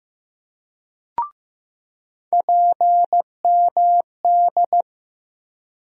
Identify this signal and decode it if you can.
Morse code, heard as a keyed tone, about 15 wpm, 700 Hz. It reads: PMD
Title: USA Call Sign Postfixes - 3 Letters 15wpm